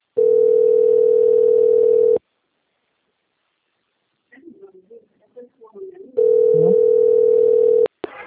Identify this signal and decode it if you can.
Ringback tone (North America), 440 + 480 Hz, with speech.
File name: out- - - - - .44870